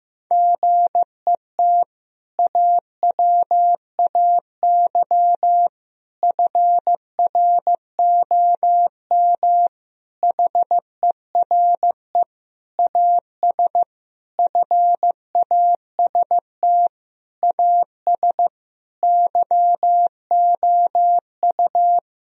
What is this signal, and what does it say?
Morse code, heard as a keyed tone, about 16 wpm, 700 Hz. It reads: GET AWAY FROM HERE AS FAST AS YOU
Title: Sentences from Top 1000 Words - Review Part 2 - 15wpm